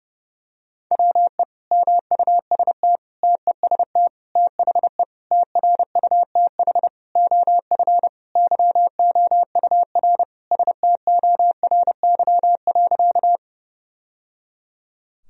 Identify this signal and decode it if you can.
Morse code, heard as a keyed tone, about 30 wpm, 700 Hz. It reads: WE MUST TEST THE TRUTH OF YOUR STORY.
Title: Sentences from Top 1000 Words - Review Part 3 - 30wpm